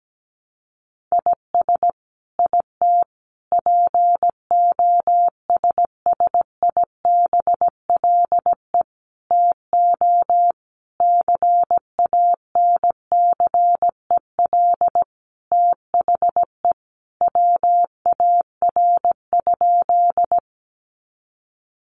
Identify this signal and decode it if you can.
Morse code, heard as a keyed tone, about 17 wpm, 700 Hz. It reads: IS IT POSSIBLE TO CANCEL THE WAR?
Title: Sentences from Top 1200 Words 17wpm